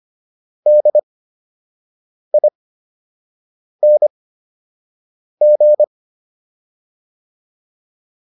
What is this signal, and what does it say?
Morse code, heard as a keyed tone, about 25 wpm, spaced as for 6 wpm, 600 Hz. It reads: DING